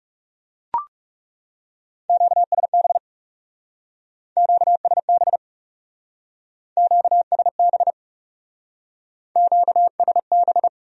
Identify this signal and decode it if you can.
Morse code, heard as a keyed tone, about 38 wpm, 700 Hz. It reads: QSB QSB QSB QSB